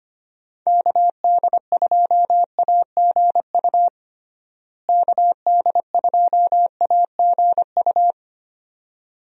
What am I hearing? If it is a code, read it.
Morse code, heard as a keyed tone, about 25 wpm, 700 Hz. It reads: KD2AGU KD2AGU
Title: USA Call Signs - 6 Characters - Repeated 2x - 25wpm